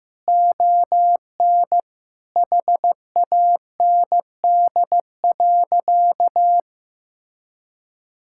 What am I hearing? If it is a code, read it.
Morse code, heard as a keyed tone, about 15 wpm, 700 Hz. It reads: ON HAND.
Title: Sentences from Top 1000 Words - Review Part 5 - 15wpm